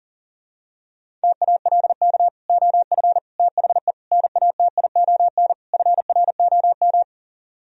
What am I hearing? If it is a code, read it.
Morse code, heard as a keyed tone, about 37 wpm, 700 Hz. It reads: TALK OF THE NATION FROM
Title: Speed-Racing - Sets of 5 Words 40wpm